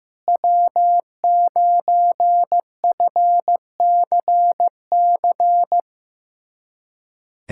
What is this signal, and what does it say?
Morse code, heard as a keyed tone, about 16 wpm, 700 Hz. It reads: W9FCC
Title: USA Callsign DE Callsign - 5 Characters - 15wpm